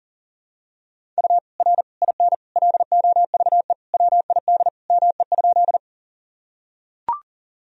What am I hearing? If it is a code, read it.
Morse code, heard as a keyed tone, about 40 wpm, 700 Hz. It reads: U R IN LOVE WID ME?
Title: Sentences from Top 400 Words and Abbreviations 40wpm